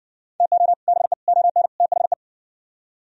Morse code, heard as a keyed tone, about 59 wpm, 700 Hz. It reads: TO BE ON THE